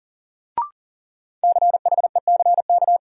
Morse code, heard as a keyed tone, about 40 wpm, 700 Hz. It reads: CHECK
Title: Sentences from Top 1000 Words - Review Part 3 - New Word Drill - 40wpm